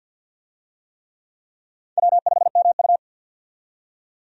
Morse code, heard as a keyed tone, about 52 wpm, 700 Hz. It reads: W5MU
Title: USA Call Signs - 4 Characters - 50wpm